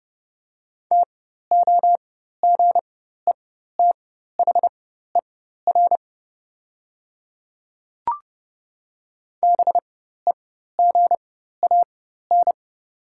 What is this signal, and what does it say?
Morse code, heard as a keyed tone, about 30 wpm, spaced as for 14 wpm, 700 Hz. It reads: TOGETHER BEGAN
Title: 4x Character Spacing - Top 300 Words - 30wpm